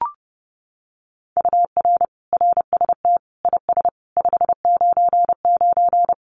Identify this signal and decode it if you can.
Morse code, heard as a keyed tone, about 30 wpm, 700 Hz. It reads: UR RST IS 599
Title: Common QSO Phrases Encore 30wpm